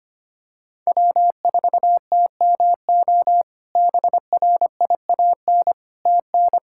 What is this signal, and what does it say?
Morse code, heard as a keyed tone, about 25 wpm, 700 Hz. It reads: W4TMO BRIAN TN
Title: K1USN SST QSO Practice - 25wpm